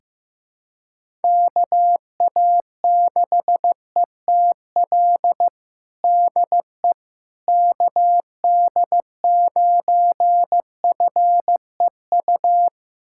Morse code, heard as a keyed tone, about 15 wpm, 700 Hz. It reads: KA6ETL DE KD9FEU
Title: USA Callsign DE Callsign - 6 Characters - 15wpm